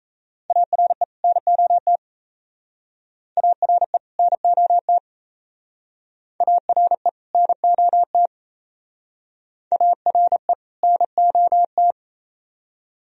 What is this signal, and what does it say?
Morse code, heard as a keyed tone, about 34 wpm, 700 Hz. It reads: ARE NOT ARE NOT ARE NOT ARE NOT
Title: Speed-Racing - Sets of 2 Words 28wpm